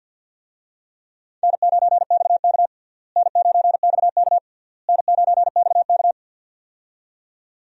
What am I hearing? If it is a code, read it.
Morse code, heard as a keyed tone, about 47 wpm, 700 Hz. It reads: N9XK N9XK N9XK
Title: USA Call Signs - 4 Characters - Repeated 3x - 50wpm